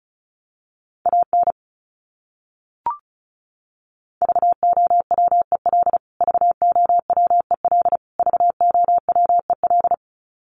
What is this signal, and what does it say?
Morse code, heard as a keyed tone, about 34 wpm, 700 Hz. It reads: AN VOWEL VOWEL VOWEL
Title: Top 500 Words Repeated 35wpm